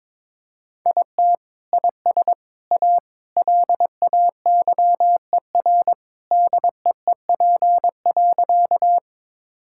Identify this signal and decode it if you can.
Morse code, heard as a keyed tone, about 21 wpm, 700 Hz. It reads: IT IS A LAYER DEEP.
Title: Sentences from Top 1600 Words 22wpm